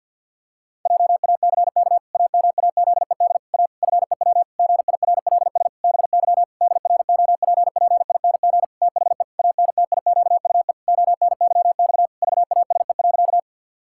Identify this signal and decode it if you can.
Morse code, heard as a keyed tone, about 51 wpm, 700 Hz. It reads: JACK AMAZED A FEW GIRLS BY DROPPING THE ANTIQUE ONYX VASE.
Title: Pangram Challenge v2 - 50wpm